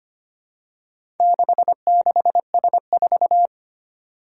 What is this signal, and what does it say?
Morse code, heard as a keyed tone, about 25 wpm, 700 Hz. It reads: 66S4
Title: Four Challenging Letter-Number Combinations 25wpm